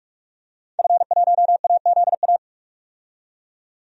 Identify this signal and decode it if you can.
Morse code, heard as a keyed tone, about 43 wpm, 700 Hz. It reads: F1AZA
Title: DXCC-100 Call Signs - 5 Characters - 45wpm